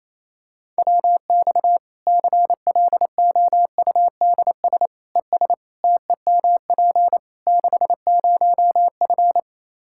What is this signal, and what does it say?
Morse code, heard as a keyed tone, about 28 wpm, 700 Hz. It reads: WX CLOUDS ES TEMP 60F
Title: Practice QSO - Exchange 2 - Info - 28wpm